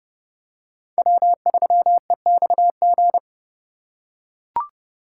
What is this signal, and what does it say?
Morse code, heard as a keyed tone, about 27 wpm, 700 Hz. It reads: W3EXG